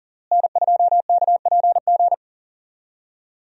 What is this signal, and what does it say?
Morse code, heard as a keyed tone, about 37 wpm, 700 Hz. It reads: N2KPG
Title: USA Call Signs - 5 Characters - Repeated 2x - 40wpm